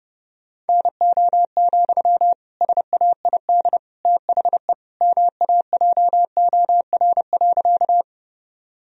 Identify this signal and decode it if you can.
Morse code, heard as a keyed tone, about 30 wpm, 700 Hz. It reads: NO, SAID THE MAJOR.
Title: Sentences from Top 1000 Words - Review Part 4 - 30wpm